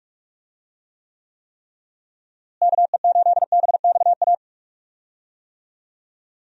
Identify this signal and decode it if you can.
Morse code, heard as a keyed tone, about 46 wpm, 700 Hz. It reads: KE8BXA
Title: USA Call Signs - 6 Characters - Repeated 3x - 45wpm